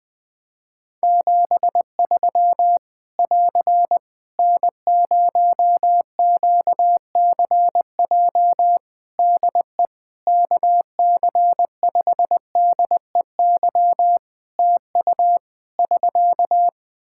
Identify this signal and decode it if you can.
Morse code, heard as a keyed tone, about 21 wpm, 700 Hz. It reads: 73 <AR> N0QCJ DE KC5DEY TU <SK>
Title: Practice QSO - Exchange 4 - Ending - 20wpm